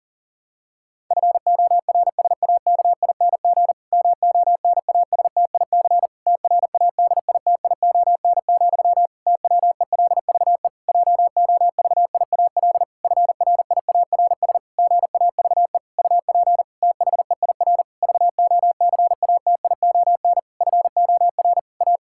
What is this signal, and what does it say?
Morse code, heard as a keyed tone, about 40 wpm, 700 Hz. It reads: FORSAKING MONASTIC TRADITION, TWELVE JOVIAL FRIARS GAVE UP THEIR VOCATION FOR A
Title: Pangram Challenge v2 - 40wpm